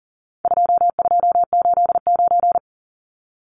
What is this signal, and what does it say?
Morse code, heard as a keyed tone, about 40 wpm, 700 Hz. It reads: 2289
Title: Numbers - 4-Digit 40wpm